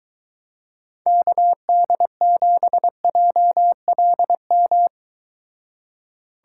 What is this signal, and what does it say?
Morse code, heard as a keyed tone, about 23 wpm, 700 Hz. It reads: KD7JLM